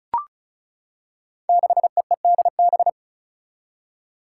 Morse code, heard as a keyed tone, about 36 wpm, 700 Hz. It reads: 6EEDB
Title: Five Challenging Letter-Number Combinations 35wpm